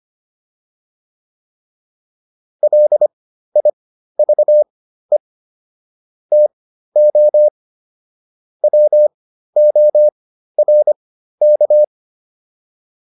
Morse code, heard as a keyed tone, about 25 wpm, spaced as for 13 wpm, 600 Hz. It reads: LIVE TO WORK